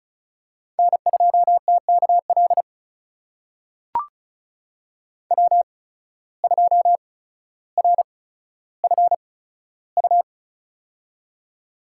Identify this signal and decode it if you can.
Morse code, heard as a keyed tone, about 36 wpm, spaced as for 15 wpm, 700 Hz. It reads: N2TKL W2RFU
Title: USA Call Signs - 5 Characters - 8x Character Spacing - 35wpm